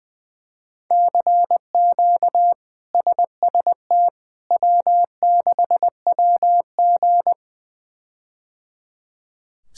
Morse code, heard as a keyed tone, about 20 wpm, 700 Hz. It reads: CQ SST W6WG
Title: K1USN SST QSO Practice - 20wpm